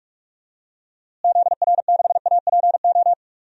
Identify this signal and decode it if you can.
Morse code, heard as a keyed tone, about 46 wpm, 700 Hz. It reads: ZR6APO
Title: DXCC-100 Call Signs - 6 Characters - Repeated 3x - 45wpm